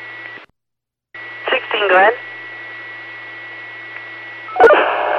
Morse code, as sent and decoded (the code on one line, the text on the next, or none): none